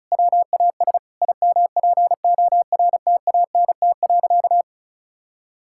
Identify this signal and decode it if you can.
Morse code, heard as a keyed tone, about 36 wpm, 700 Hz. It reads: WAS IMPORTANT.
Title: Sentences from Top 1000 Words - Review Part 5 - 35wpm